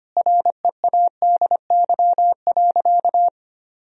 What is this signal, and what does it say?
Morse code, heard as a keyed tone, about 25 wpm, 700 Hz. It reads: READY.
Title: Sentences from Top 1000 Words - Review Part 3 - 25wpm